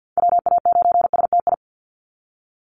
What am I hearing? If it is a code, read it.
Morse code, heard as a keyed tone, about 50 wpm, 700 Hz. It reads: WA9STI
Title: NA SOTA Activator Callsigns 50wpm